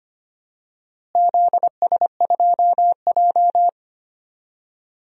Morse code, heard as a keyed tone, about 25 wpm, 700 Hz. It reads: ZS2J
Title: DXCC-100 Call Signs - 4 Characters - Repeated 3x - 25wpm